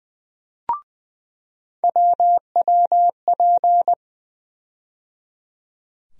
Morse code, heard as a keyed tone, about 20 wpm, 700 Hz. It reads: WWP
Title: USA Call Sign Postfixes - 3 Letters 20wpm